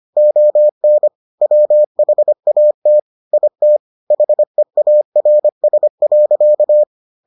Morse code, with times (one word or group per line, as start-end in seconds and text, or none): 0.2-1.1 ON
1.4-3.0 WHAT
3.3-3.8 IT
4.1-6.8 HEARS.